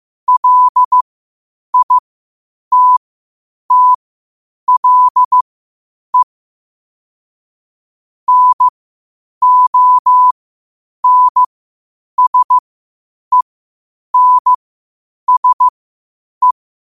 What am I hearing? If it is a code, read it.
Morse code, heard as a keyed tone, about 15 wpm, spaced as for 8 wpm, 1000 Hz. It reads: LITTLE NONSENSE
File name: qotdfiles / 8wpm